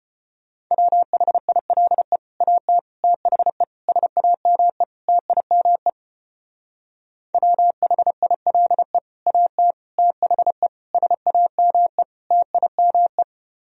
Morse code, heard as a keyed tone, about 30 wpm, 700 Hz. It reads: WHILE AT THE SAME TIME WHILE AT THE SAME TIME
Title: Speed-Racing - Sets of 5 Words 25wpm